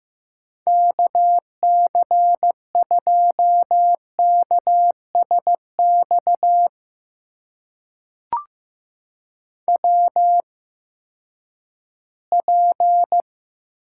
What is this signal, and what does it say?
Morse code, heard as a keyed tone, about 15 wpm, spaced as for 10 wpm, 700 Hz. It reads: KC2KSX WP
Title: USA Call Signs - 8x Character Spacing - 15wpm